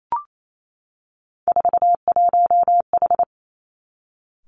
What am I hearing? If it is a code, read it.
Morse code, heard as a keyed tone, about 28 wpm, 700 Hz. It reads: 41H
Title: Three Challenging Letter-Number Combinations 28wpm